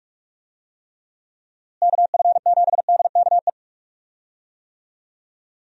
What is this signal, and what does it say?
Morse code, heard as a keyed tone, about 46 wpm, 700 Hz. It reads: KF7DKE
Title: USA Call Signs - 6 Characters - Repeated 3x - 45wpm